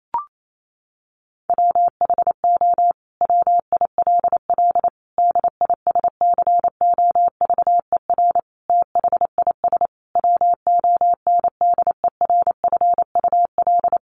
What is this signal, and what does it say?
Morse code, heard as a keyed tone, about 28 wpm, 700 Hz. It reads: WHO WILL DISCOVER THIS WONDERFUL
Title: Sentences from Top 1100 Words 28wpm